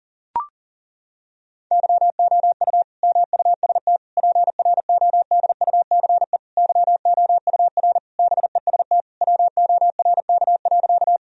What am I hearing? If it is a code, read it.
Morse code, heard as a keyed tone, about 40 wpm, 700 Hz. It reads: YOU MUST PRODUCE YOUR BEST WORK.
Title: Sentences from Top 1000 Words - Review Part 3 - 40wpm